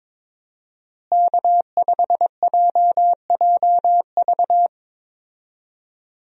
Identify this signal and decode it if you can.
Morse code, heard as a keyed tone, about 22 wpm, 700 Hz. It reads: K5JJV
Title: USA Call Signs - 5 Characters - Repeated 2x - 22wpm